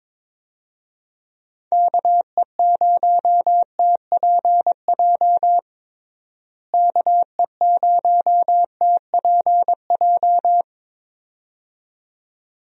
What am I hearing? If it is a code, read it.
Morse code, heard as a keyed tone, about 22 wpm, 700 Hz. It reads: KE0TPJ KE0TPJ
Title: USA Call Signs - 6 Characters - Repeated 2x - 22wpm